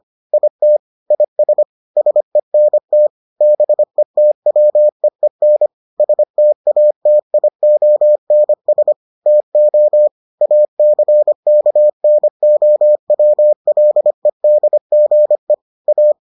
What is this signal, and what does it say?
Morse code, heard as a keyed tone, about 25 wpm, 600 Hz. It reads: IT IS SENT BETWEEN STATIONS TO ACKNOWLEDGE A